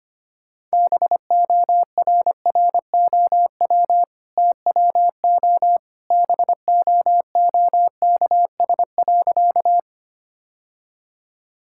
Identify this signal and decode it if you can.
Morse code, heard as a keyed tone, about 25 wpm, 690 Hz. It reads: BORROW TWO BOOKS.
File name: Easy Sentences with Hints - 25wpm